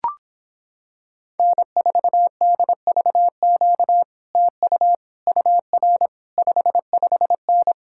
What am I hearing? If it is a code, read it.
Morse code, heard as a keyed tone, about 26 wpm, 700 Hz. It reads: N4DVQ TU UR 55N